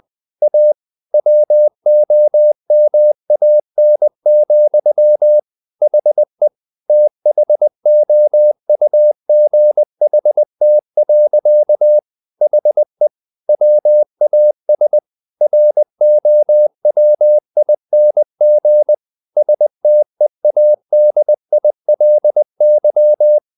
Morse code, heard as a keyed tone, about 20 wpm, 600 Hz. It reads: A WOMAN, HE THOUGHT. HE WAS ROWING STEADILY
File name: the old man and the sea / the old man and the sea part 1 20wpm